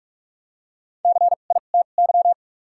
Morse code, heard as a keyed tone, about 47 wpm, spaced as for 33 wpm, 700 Hz. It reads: CITY
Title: 2x Character Spacing - Top 500 Words - 45wpm